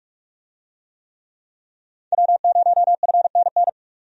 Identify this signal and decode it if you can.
Morse code, heard as a keyed tone, about 44 wpm, 700 Hz. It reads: W0FNN
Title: USA Call Signs - 5 Characters - 45wpm